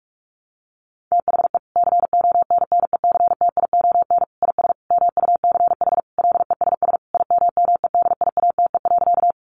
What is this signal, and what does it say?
Morse code, heard as a keyed tone, about 46 wpm, 700 Hz. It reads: THE CONNECTION IS MUCH LESS IMMEDIATE.
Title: Sentences from Top 1300 Words 45wpm